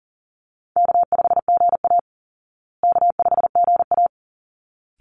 Morse code, heard as a keyed tone, about 40 wpm, 700 Hz. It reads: K5GA K5GA
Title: Easy Call Signs Repeated 2x - 40wpm